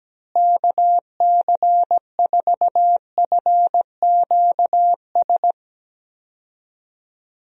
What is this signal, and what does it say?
Morse code, heard as a keyed tone, about 17 wpm, 700 Hz. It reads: KC4FQS
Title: USA Call Signs - 6 Characters - Repeated 3x - 17wpm